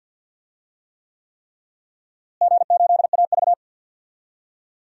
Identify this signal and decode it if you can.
Morse code, heard as a keyed tone, about 52 wpm, 700 Hz. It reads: G8AV
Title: DXCC-100 Call Signs - 4 Characters - Repeated 3x - 50wpm